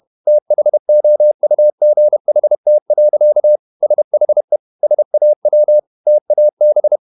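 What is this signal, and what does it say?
Morse code, heard as a keyed tone, about 31 wpm, 600 Hz. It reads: THOUGHT. SHE SAW TAB
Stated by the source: Morse code at 30 wpm, the novel Never by Ken Follett in English, from CW med Gnister.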